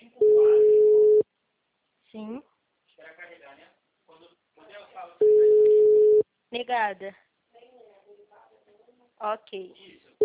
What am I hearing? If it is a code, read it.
Ringback tone (Europe, 425 Hz), with speech.
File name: out- -2003- - - .34407